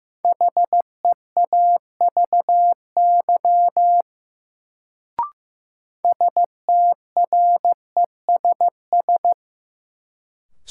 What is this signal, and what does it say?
Morse code, heard as a keyed tone, about 15 wpm, 700 Hz. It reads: HEAVY STRESS
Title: Sentences from Top 1000 Words - Review Part 6 - New Word Drill - 15wpm